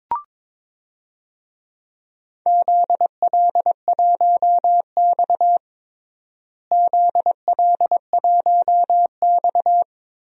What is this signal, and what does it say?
Morse code, heard as a keyed tone, about 22 wpm, 700 Hz. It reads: ZL1X ZL1X
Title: DXCC-100 Call Signs - 4 Characters - Repeated 3x - 22wpm